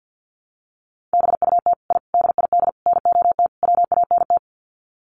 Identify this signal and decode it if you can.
Morse code, heard as a keyed tone, about 50 wpm, 700 Hz. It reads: BUT I DID NOT WANT
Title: Sets of 5 Words Encore 50wpm